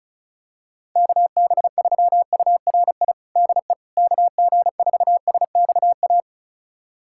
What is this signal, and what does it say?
Morse code, heard as a keyed tone, about 36 wpm, 700 Hz. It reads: KB3URI DE KG4SXA
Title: USA Callsign DE Callsign 35wpm